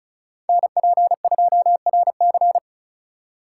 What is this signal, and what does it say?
Morse code, heard as a keyed tone, about 36 wpm, 700 Hz. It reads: NP2RC
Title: USA Call Signs - 4x Character Spacing - 35wpm